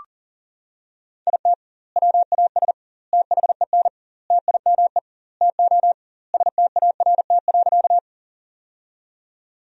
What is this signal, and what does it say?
Morse code, heard as a keyed tone, about 40 wpm, 700 Hz. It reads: IT WAS THEN TIME TO START.
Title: 2x Word-Spacing - Sentences from Top 300 Words - 40wpm